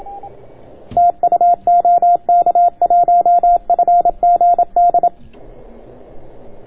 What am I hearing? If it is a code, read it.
Morse code, heard as a keyed tone, about 27 wpm, 690 Hz. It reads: TU OK1FGD